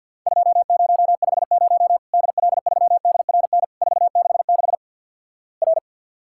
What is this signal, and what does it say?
Morse code, heard as a keyed tone, about 53 wpm, 700 Hz. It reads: 2050 DL3DRN 466 R
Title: Worked All Europe DX Contest - QTC Exchange with Repeats - 50wpm